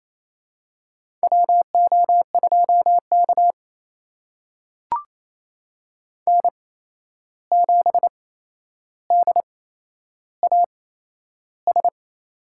Morse code, heard as a keyed tone, about 28 wpm, spaced as for 11 wpm, 700 Hz. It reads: WO2K N7DAS
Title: USA Call Signs - 8x Character Spacing - 28wpm